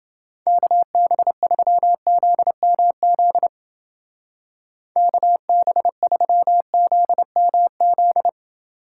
Morse code, heard as a keyed tone, about 29 wpm, 700 Hz. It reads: KB3ZMZ KB3ZMZ